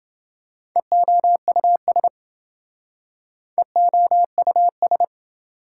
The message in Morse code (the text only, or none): EOUS EOUS